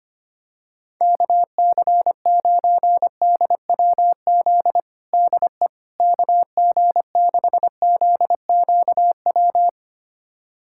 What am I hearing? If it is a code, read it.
Morse code, heard as a keyed tone, about 25 wpm, 700 Hz. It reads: KC9DWZ DE KG6ZQW
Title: USA Callsign DE Callsign - 6 Characters - 25wpm